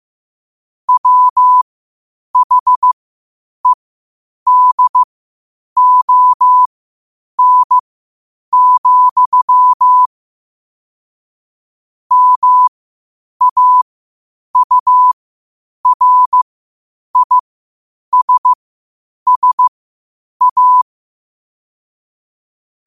Morse code, heard as a keyed tone, about 15 wpm, spaced as for 8 wpm, 1000 Hz. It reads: WHEDON, MAURISSA
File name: qotdfiles / 8wpm